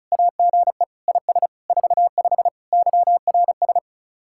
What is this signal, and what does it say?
Morse code, heard as a keyed tone, about 45 wpm, 700 Hz. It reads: AGE IS 45 YRS